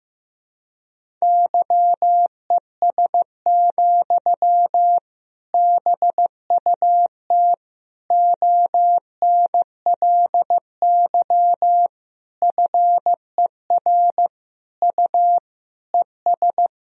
Morse code, heard as a keyed tone, about 15 wpm, 700 Hz. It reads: YES, BUT ONLY FER U ES
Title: Sentences from Top 500 Words and Abbreviations 15wpm